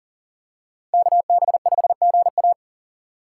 Morse code, heard as a keyed tone, about 40 wpm, 700 Hz. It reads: KB5GA
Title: K1USN SST QSO Practice - 40wpm